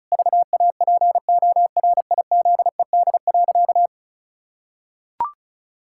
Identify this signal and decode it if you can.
Morse code, heard as a keyed tone, about 36 wpm, 700 Hz. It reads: VAPORIZED.
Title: Pangram Challenge v2 - 35wpm